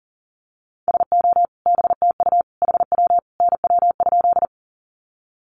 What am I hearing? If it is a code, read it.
Morse code, heard as a keyed tone, about 40 wpm, 700 Hz. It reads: SO BTU HW NW?
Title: Common QSO Phrases 40wpm